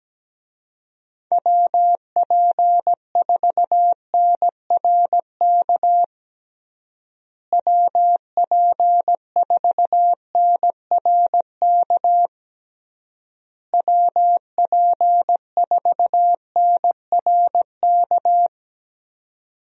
Morse code, heard as a keyed tone, about 17 wpm, 700 Hz. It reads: WP4NRK WP4NRK WP4NRK
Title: USA Call Signs - 6 Characters - Repeated 3x - 17wpm